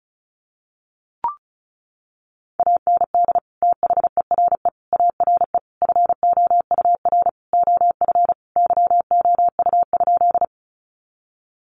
Morse code, heard as a keyed tone, about 34 wpm, 700 Hz. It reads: AND THERE ARE FOUR OF YOU?
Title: Sentences from Top 300 Words 35wpm